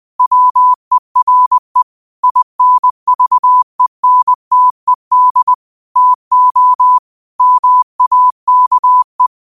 Morse code, heard as a keyed tone, about 20 wpm, 1000 Hz. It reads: WERE INVENTED TO MAKE
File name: qotdfiles / 20wpm